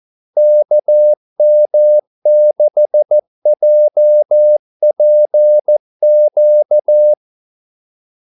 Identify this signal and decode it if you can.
Morse code, heard as a keyed tone, about 14 wpm, 600 Hz. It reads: KM6JPQ
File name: Callsigns-14wpm